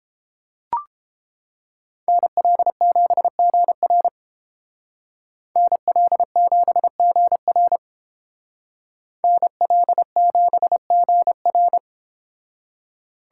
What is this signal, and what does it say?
Morse code, heard as a keyed tone, about 30 wpm, 700 Hz. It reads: NL7GR NL7GR NL7GR